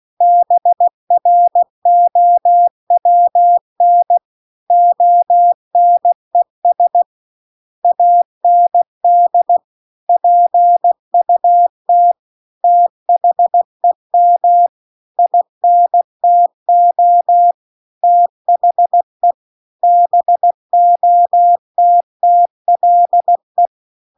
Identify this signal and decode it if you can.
Morse code, heard as a keyed tone, about 16 wpm, 700 Hz. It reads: BROWN ONES AND PUT THEM INTO THE BOTTLE
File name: CW-The-Big-Twoheaded-River-2-16WPM